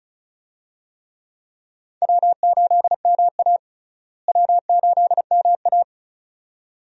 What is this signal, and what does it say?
Morse code, heard as a keyed tone, about 36 wpm, 700 Hz. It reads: W8MA W8MA
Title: USA Call Signs - 4 Characters - Repeated 2x - 35wpm